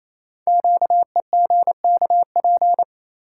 Morse code, heard as a keyed tone, about 29 wpm, 700 Hz. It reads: QEGKP
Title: Code Groups - 5 Letters - 28wpm